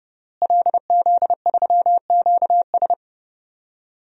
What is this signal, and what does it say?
Morse code, heard as a keyed tone, about 30 wpm, 700 Hz. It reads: LZ3QS
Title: DXCC-100 Call Signs - 5 Characters - Repeated 3x - 30wpm